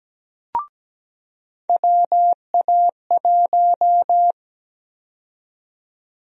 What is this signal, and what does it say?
Morse code, heard as a keyed tone, about 17 wpm, 700 Hz. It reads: WA1
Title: USA Call Sign Prefixes - 17wpm